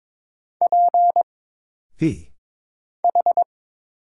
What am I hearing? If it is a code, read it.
Morse code, heard as a keyed tone, about 22 wpm, 700 Hz. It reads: P H
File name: Single Character (Letter:Number:Punctuation:Prosign) - Mind-Melt - 22wpm